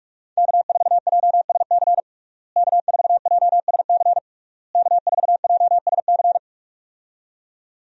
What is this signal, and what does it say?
Morse code, heard as a keyed tone, about 46 wpm, 700 Hz. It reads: K4JSC K4JSC K4JSC
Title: USA Call Signs - 5 Characters - Repeated 3x - 45wpm